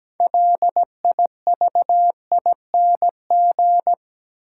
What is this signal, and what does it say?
Morse code, heard as a keyed tone, about 17 wpm, 700 Hz. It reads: LIVING
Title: Sets of 4 Words - New Word Drill - 17wpm